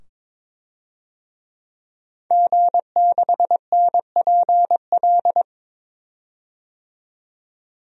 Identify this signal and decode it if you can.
Morse code, heard as a keyed tone, about 22 wpm, 700 Hz. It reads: G6NPL
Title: DXCC-100 Call Signs - 5 Characters - 22wpm